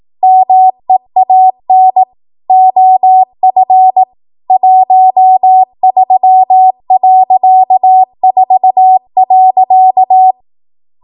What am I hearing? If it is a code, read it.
Morse code, heard as a keyed tone, about 18 wpm, 750 Hz. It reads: MEAN OF 13.4.
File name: Archive / 18 WPM / 18WPM